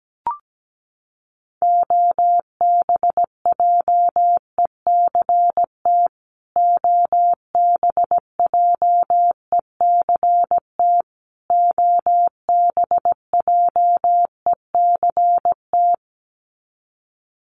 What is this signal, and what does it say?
Morse code, heard as a keyed tone, about 17 wpm, 700 Hz. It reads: OBJECT OBJECT OBJECT
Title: Top 500 Words Repeated 17wpm